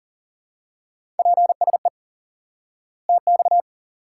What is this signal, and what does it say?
Morse code, heard as a keyed tone, about 37 wpm, 700 Hz. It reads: PSE TX
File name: IQE Course 3 - Lesson 89 - YAGI - Two Elements - 40wpm